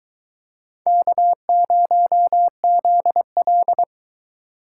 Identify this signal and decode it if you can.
Morse code, heard as a keyed tone, about 22 wpm, 700 Hz. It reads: K0ZL